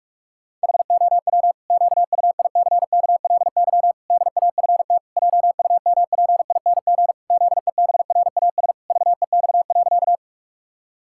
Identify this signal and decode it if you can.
Morse code, heard as a keyed tone, about 46 wpm, 700 Hz. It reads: HOW QUICKLY DAFT JUMPING ZEBRAS VEX.
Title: Pangram Challenge v2 - 45wpm